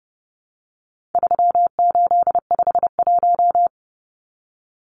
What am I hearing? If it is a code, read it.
Morse code, heard as a keyed tone, about 30 wpm, 700 Hz. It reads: 3851
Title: Numbers - 4-Digit 30wpm